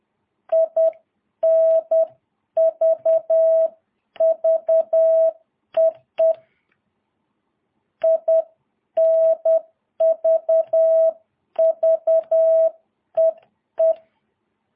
Morse code, heard as a keyed tone, about 10 wpm, 650 Hz. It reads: INV VEE INV VEE
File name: jh8cbh / 2022cw